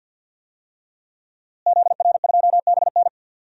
Morse code, heard as a keyed tone, about 51 wpm, 700 Hz. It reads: ZR2BN
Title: DXCC-100 Call Signs - 5 Characters - Repeated 3x - 50wpm